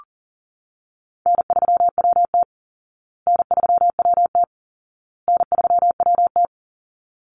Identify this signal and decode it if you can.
Morse code, heard as a keyed tone, about 40 wpm, 700 Hz. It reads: N3WT N3WT N3WT
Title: Easy Call Signs Repeated 3x - 40wpm